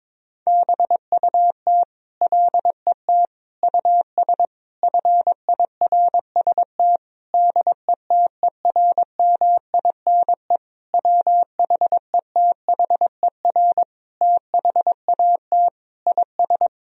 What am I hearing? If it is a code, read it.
Morse code, heard as a keyed tone, about 22 wpm, 700 Hz. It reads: BUT LET US FIRST DETERMINE WHETHER THAT IS
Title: Sentences from Top 1000 Words - Review Part 4 - 22wpm